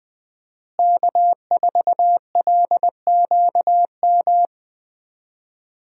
Morse code, heard as a keyed tone, about 19 wpm, 700 Hz. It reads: K4LQM
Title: USA Call Signs - 5 Characters - Repeated 3x - 20wpm